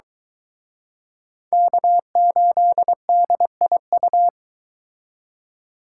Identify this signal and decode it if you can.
Morse code, heard as a keyed tone, about 23 wpm, 700 Hz. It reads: K8DIU